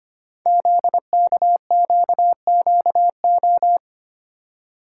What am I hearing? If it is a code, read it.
Morse code, heard as a keyed tone, about 26 wpm, 700 Hz. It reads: ZKQQO
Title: Code Groups - 5 Letters - 25wpm